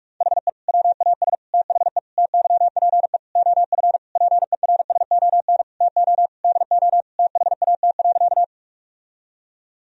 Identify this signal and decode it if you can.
Morse code, heard as a keyed tone, about 46 wpm, 700 Hz. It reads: HE WAS THE TYPE OF PERSON TO DO THAT.
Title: Sentences from Top 1000 Words - Review Part 3 - 45wpm